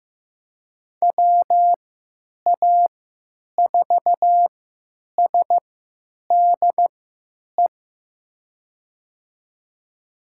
Morse code, heard as a keyed tone, about 15 wpm, spaced as for 9 wpm, 700 Hz. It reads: WA4SDE
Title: USA Call Signs - 6 Characters - 3x Character Spacing - 15wpm